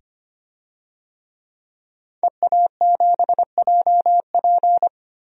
Morse code, heard as a keyed tone, about 25 wpm, 700 Hz. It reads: EA7JP
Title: DXCC-100 Call Signs - 5 Characters - 25wpm